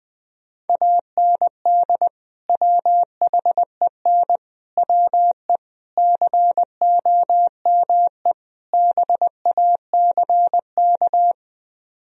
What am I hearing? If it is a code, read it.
Morse code, heard as a keyed tone, about 20 wpm, 700 Hz. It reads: AND WHEN WE COME BACK